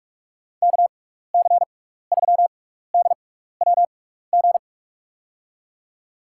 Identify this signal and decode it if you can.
Morse code, heard as a keyed tone, about 44 wpm, spaced as for 15 wpm, 700 Hz. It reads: KC3DWG
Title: USA Call Signs - 6 Characters - 6x Character Spacing - 45wpm